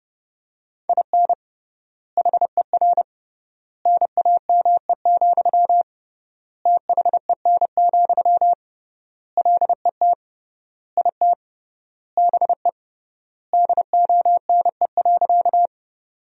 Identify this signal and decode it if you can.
Morse code, heard as a keyed tone, about 30 wpm, 700 Hz. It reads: IN HER NAME, THEN, LET IT BE DONE.
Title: 3x Word-Spacing - Sentences from Top 500 Words - 30wpm